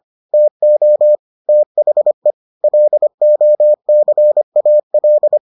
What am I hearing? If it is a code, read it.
Morse code, heard as a keyed tone, about 25 wpm, 600 Hz. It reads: TO THE LOCAL